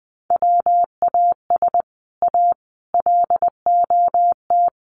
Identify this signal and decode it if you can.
Morse code, heard as a keyed tone, about 20 wpm, 700 Hz. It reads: WAS A LOT
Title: Sets of 4 Words 20wpm